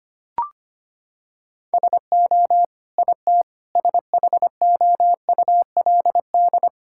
Morse code, heard as a keyed tone, about 25 wpm, 700 Hz. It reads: SO IT SHOULD
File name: Sets of 9 Words - 25wpm